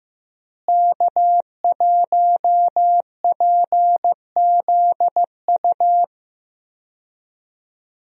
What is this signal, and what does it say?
Morse code, heard as a keyed tone, about 16 wpm, 700 Hz. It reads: K1PZU
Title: USA Call Signs - 5 Characters - Repeated 2x - 15wpm